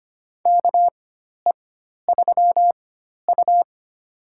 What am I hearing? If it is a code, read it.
Morse code, heard as a keyed tone, about 25 wpm, spaced as for 12 wpm, 700 Hz. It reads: KE3U
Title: USA Call Signs - 4 Characters - 4x Character Spacing - 25wpm